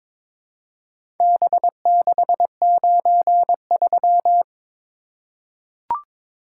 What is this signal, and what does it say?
Morse code, heard as a keyed tone, about 22 wpm, 700 Hz. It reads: B693
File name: practice / 22wpm / Four Letter-Number Combinations - 22wpm